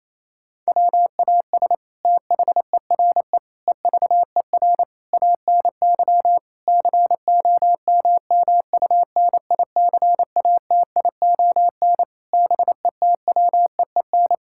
Morse code, heard as a keyed tone, about 27 wpm, 700 Hz. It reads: WAS THERE EVER ANY COMMUNICATION BETWEEN
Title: Sentences from Top 1000 Words - Review Part 6 - 28wpm